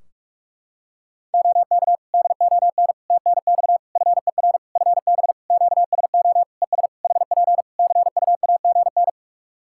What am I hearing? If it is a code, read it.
Morse code, heard as a keyed tone, about 41 wpm, 700 Hz. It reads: OK DON TNX FER FB QSO ES HP CUAGN
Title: Practice QSO - Exchange 4 - Ending - 45wpm